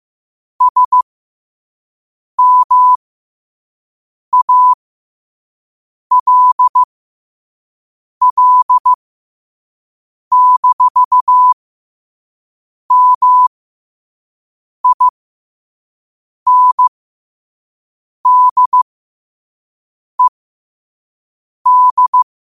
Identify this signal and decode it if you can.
Morse code, heard as a keyed tone, about 15 wpm, spaced as for 5 wpm, 1000 Hz. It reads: SMALL-MINDED
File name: qotdfiles / 5wpm